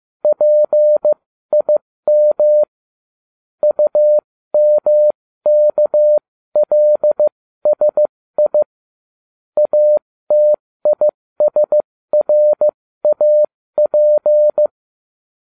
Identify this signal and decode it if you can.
Morse code, heard as a keyed tone, about 15 wpm, spaced as for 13 wpm, 600 Hz. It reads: PIM UMKLSI ATISRAP